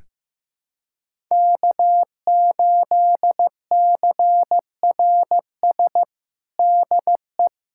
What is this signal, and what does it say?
Morse code, heard as a keyed tone, about 15 wpm, 700 Hz. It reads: K8CRS DE
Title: USA Callsign DE Callsign - 5 Characters - 15wpm